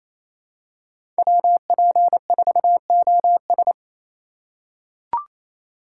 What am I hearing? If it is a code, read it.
Morse code, heard as a keyed tone, about 28 wpm, 700 Hz. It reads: WP4OS
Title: USA Call Signs - 5 Characters - 4x Character Spacing - 28wpm